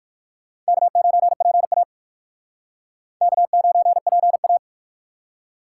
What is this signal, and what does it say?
Morse code, heard as a keyed tone, about 55 wpm, 700 Hz. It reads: K9PA K9PA
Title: Speed Racing - USA Call Signs 45wpm